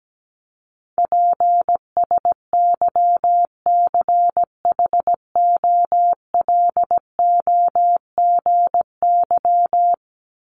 Morse code, heard as a keyed tone, about 17 wpm, 700 Hz. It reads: PSYCHOLOGY
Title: Top 1251-1500 Words 17wpm